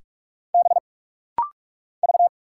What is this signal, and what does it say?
Morse code, heard as a keyed tone, about 44 wpm, 700 Hz. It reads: B V
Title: Single Letter-Number - Rapid-Fire 45wpm